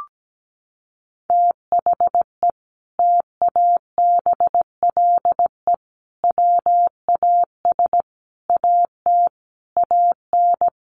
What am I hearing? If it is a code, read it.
Morse code, heard as a keyed tone, about 17 wpm, 700 Hz. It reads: THE TABLE WAS AT AN
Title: Sentences from Top 1500 Words 17wpm